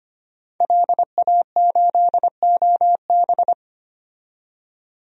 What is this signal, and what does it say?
Morse code, heard as a keyed tone, about 25 wpm, 700 Hz. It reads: LA8OB
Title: DXCC-100 Call Signs - 5 Characters - 25wpm